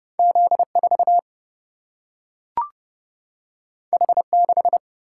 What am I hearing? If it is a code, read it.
Morse code, heard as a keyed tone, about 30 wpm, 700 Hz. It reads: Z4 H6
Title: Single-Letter Single-Number 30wpm